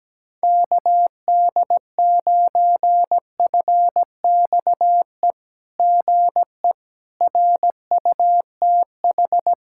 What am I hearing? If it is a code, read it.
Morse code, heard as a keyed tone, about 17 wpm, 700 Hz. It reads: KD9FXE GE RUTH
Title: SOTA QSO Practice 17wpm